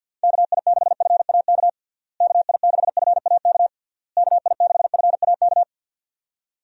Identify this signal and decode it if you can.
Morse code, heard as a keyed tone, about 45 wpm, 700 Hz. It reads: KI6FAK KI6FAK KI6FAK
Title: USA Call Signs - 6 Characters - Repeated 3x - 50wpm